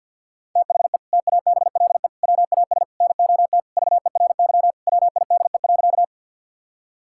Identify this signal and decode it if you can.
Morse code, heard as a keyed tone, about 52 wpm, 700 Hz. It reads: THE TABLE WAS NOT VERY WIDE.
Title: Sentences from Top 1000 Words - Review Part 6 - 50wpm